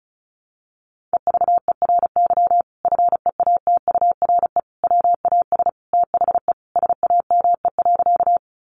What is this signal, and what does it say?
Morse code, heard as a keyed tone, about 34 wpm, 700 Hz. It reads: EVERY FEATURE WAS THE SAME.
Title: Sentences from Top 800 Words 35wpm